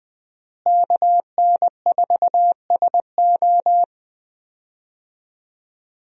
Morse code, heard as a keyed tone, about 20 wpm, 700 Hz. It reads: KN4SO
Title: USA Call Signs - 5 Characters - Repeated 3x - 20wpm